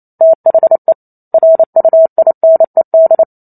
Morse code, heard as a keyed tone, about 28 wpm, 650 Hz. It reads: THE RUINED